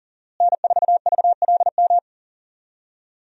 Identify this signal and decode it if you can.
Morse code, heard as a keyed tone, about 40 wpm, 700 Hz. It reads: N4VLM
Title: USA Call Signs - 5 Characters - Repeated 2x - 40wpm